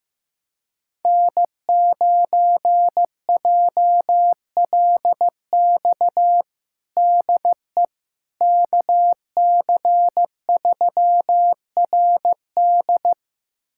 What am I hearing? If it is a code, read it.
Morse code, heard as a keyed tone, about 15 wpm, 700 Hz. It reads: N9JLX DE KC3RD
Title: USA Callsign DE Callsign 15wpm